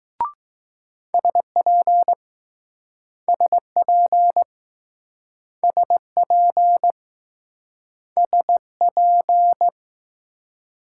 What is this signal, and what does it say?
Morse code, heard as a keyed tone, about 20 wpm, 700 Hz. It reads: SP SP SP SP